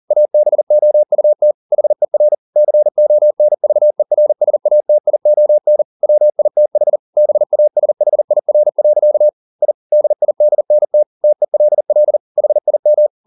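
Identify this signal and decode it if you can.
Morse code, heard as a keyed tone, about 40 wpm, 600 Hz. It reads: ABOUT HER CONVERSATION WITH BASHIR. I DIDNT TELL HIM
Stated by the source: Morse code at 40 wpm, practice text, CW med Gnister.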